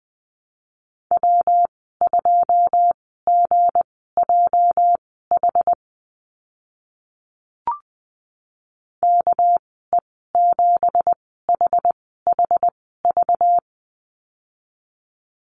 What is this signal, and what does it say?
Morse code, heard as a keyed tone, about 20 wpm, spaced as for 14 wpm, 700 Hz. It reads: W2GJH KE7HHV
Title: 207 Lesson 33g - Letter j - Call Signs 20wpm